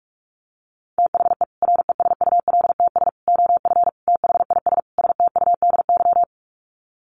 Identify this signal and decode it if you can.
Morse code, heard as a keyed tone, about 44 wpm, 700 Hz. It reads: THE RESULTS OF THIS STUDY
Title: Sets of 5 Words 45wpm